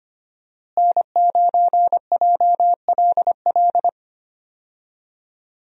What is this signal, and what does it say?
Morse code, heard as a keyed tone, about 25 wpm, 700 Hz. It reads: N9JLL
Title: USA Call Signs - 5 Characters - Repeated 3x - 25wpm